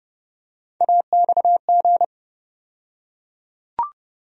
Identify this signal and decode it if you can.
Morse code, heard as a keyed tone, about 30 wpm, 700 Hz. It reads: AXG
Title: USA Call Sign Postfixes - 3 Letters 30wpm